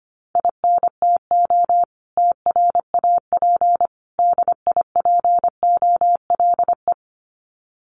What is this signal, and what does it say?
Morse code, heard as a keyed tone, about 25 wpm, 700 Hz. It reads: INTO TRAP DIPOLE
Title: Common QSO Phrases 25wpm